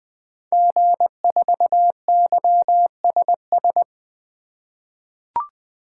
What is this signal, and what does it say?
Morse code, heard as a keyed tone, about 20 wpm, 700 Hz. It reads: G4YSS
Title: 2023 SOTA Activator Call Signs - 20wpm